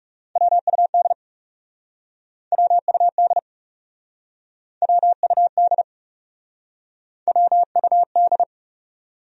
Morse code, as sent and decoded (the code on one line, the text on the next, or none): .-- ..- -.. / .-- ..- -.. / .-- ..- -.. / .-- ..- -..
WUD WUD WUD WUD